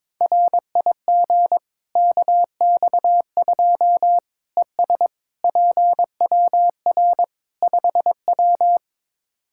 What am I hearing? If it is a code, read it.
Morse code, heard as a keyed tone, about 22 wpm, 700 Hz. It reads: RIG KX2 ES PWR 5W
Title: K1USN SST QSO Practice 22wpm